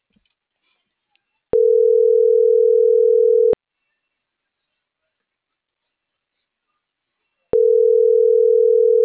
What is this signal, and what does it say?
Ringback tone (North America), 440 + 480 Hz.